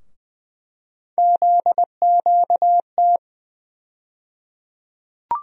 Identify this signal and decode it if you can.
Morse code, heard as a keyed tone, about 20 wpm, 700 Hz. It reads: ZQT
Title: USA Call Sign Postfixes - 3 Letters 20wpm